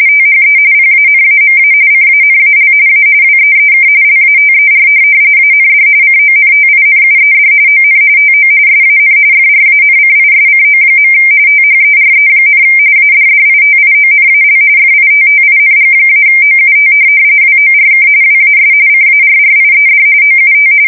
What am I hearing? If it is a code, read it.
RTTY, about 45.45 baud. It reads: THE PANAMA CANAL A WEEK LATER AND CONTINUED ON TO SAN FRANCISCO. THERE, ADDITIONAL FITTING-OUT WORK WAS CARRIED OUT
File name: fsk170 / USS-Missouri